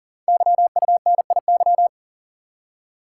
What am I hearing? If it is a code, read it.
Morse code, heard as a keyed tone, about 43 wpm, 700 Hz. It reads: YUNIY
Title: Code Groups - 5 Letters - 40wpm